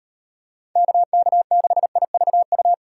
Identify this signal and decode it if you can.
Morse code, heard as a keyed tone, about 36 wpm, 700 Hz. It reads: KK6IVU